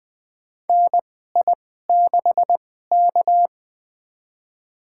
Morse code, heard as a keyed tone, about 20 wpm, spaced as for 14 wpm, 700 Hz. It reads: NI6K
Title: USA Call Signs - 4 Characters - 2x Character Spacing - 20wpm